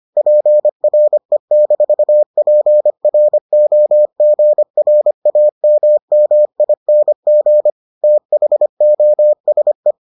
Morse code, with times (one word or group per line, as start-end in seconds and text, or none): 0.2-7.7 PRE-PROGRAMMING
8.0-9.9 THOSE